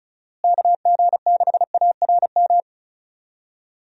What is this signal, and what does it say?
Morse code, heard as a keyed tone, about 34 wpm, 700 Hz. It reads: KG6ARM
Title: USA Call Signs - 6 Characters - Repeated 2x - 35wpm